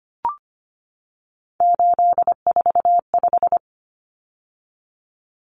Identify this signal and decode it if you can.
Morse code, heard as a keyed tone, about 25 wpm, 700 Hz. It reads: 845
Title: Common Cut Numbers - 3-Digit 25wpm